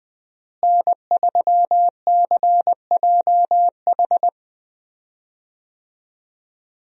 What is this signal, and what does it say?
Morse code, heard as a keyed tone, about 20 wpm, 700 Hz. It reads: N3CJH
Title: USA Call Signs - 5 Characters - Repeated 2x - 20wpm